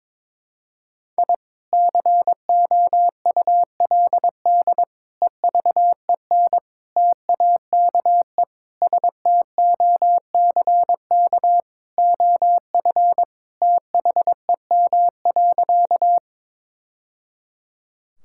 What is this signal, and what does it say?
Morse code, heard as a keyed tone, about 22 wpm, 700 Hz. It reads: I COULD EVEN TAKE STOCK OF THEM.
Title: Sentences from Top 1000 Words - Review Part 5 - 22wpm